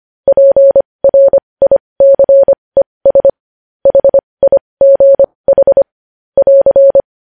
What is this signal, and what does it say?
Morse code, heard as a keyed tone, about 25 wpm, spaced as for 20 wpm, 550 Hz. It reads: PRICES HIGH <AR>
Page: news headlines in morse code at 20 wpm